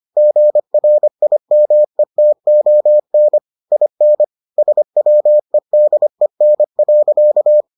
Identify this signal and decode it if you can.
Morse code, heard as a keyed tone, about 25 wpm, 600 Hz. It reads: GRIMETON IN SWEDEN.